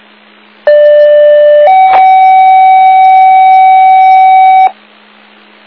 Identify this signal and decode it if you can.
Two-tone page, heard as a one-second tone, then a three-second tone, about 590 then 760 Hz, with noise. It is heard from 0.7 to 4.7 s.